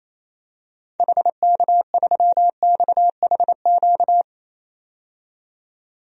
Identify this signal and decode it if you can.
Morse code, heard as a keyed tone, about 28 wpm, 700 Hz. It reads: HK3XHQ
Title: DXCC-100 Call Signs - 6 Characters - 28wpm